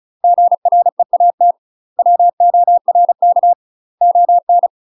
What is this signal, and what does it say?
Morse code, heard as a keyed tone, about 36 wpm, 700 Hz. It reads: GREAT WORK ON
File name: UK MASTER QSOs TRAINING / 35WPM / UK MASTER QSOs 0033